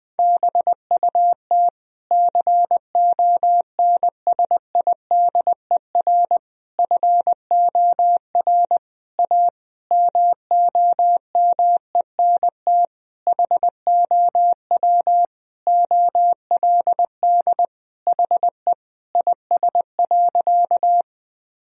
Morse code, heard as a keyed tone, about 20 wpm, 700 Hz. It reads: BUT CONSIDER FOR A MOMENT HOW OLD HE IS.
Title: Sentences from Top 1000 Words - Review Part 4 - 20wpm